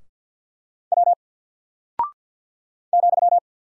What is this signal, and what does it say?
Morse code, heard as a keyed tone, about 48 wpm, 700 Hz. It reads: W ,
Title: Single Character (Letter:Number:Punctuation:Prosign) Alt 50wpm